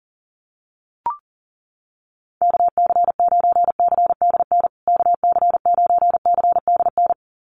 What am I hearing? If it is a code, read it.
Morse code, heard as a keyed tone, about 40 wpm, 700 Hz. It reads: KC9CDN KC9CDN
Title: USA Call Signs Repeated 2x 40wpm